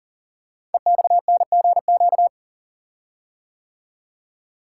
Morse code, heard as a keyed tone, about 41 wpm, 700 Hz. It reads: EXNGQ
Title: Code Groups - 5 Letters - 40wpm